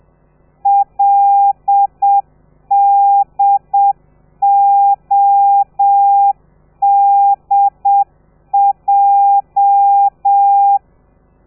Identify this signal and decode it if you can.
Morse code, heard as a keyed tone, about 7 wpm, 800 Hz. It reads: LDODJ
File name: telegrafia / i2tfj / Lez13 30